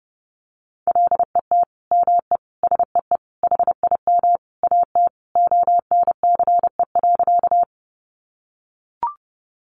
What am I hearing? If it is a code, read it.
Morse code, heard as a keyed tone, about 30 wpm, 700 Hz. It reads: LET ME SEE HIM AT ONCE.
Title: Sentences from Top 300 Words 30wpm